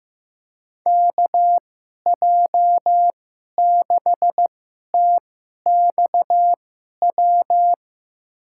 Morse code, heard as a keyed tone, about 15 wpm, spaced as for 11 wpm, 700 Hz. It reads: KJ6TXW
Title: USA Call Signs - 2x Character Spacing - 15wpm